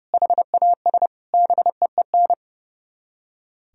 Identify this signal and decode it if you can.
Morse code, heard as a keyed tone, about 40 wpm, 700 Hz. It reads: HAS BEEN